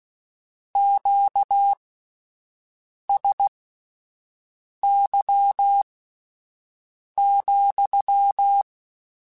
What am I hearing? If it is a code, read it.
Morse code, heard as a keyed tone, about 16 wpm, spaced as for 5 wpm, 800 Hz. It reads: QSY,
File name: Q Calls / Q Calls-mp3s / Q Calls sppc @10